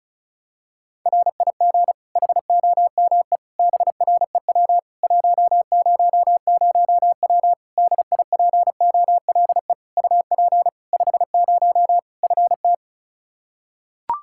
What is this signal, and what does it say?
Morse code, heard as a keyed tone, about 34 wpm, 700 Hz. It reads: RIG HOME BREW 100W DIPOLE UP 50 FT
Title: Practice QSO - Exchange 2 - Info - 35wpm